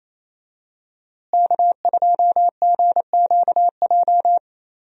Morse code, heard as a keyed tone, about 27 wpm, 700 Hz. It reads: K2GQJ
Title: USA Call Signs - 5 Characters - Repeated 2x - 28wpm